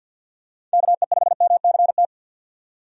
Morse code, heard as a keyed tone, about 52 wpm, 700 Hz. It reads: KE5MCT
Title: USA Call Signs - 6 Characters - 50wpm